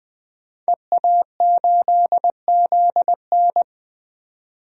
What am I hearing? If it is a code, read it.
Morse code, heard as a keyed tone, about 20 wpm, 700 Hz. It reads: EA8ZN
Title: DXCC-100 Call Signs - 5 Characters - 20wpm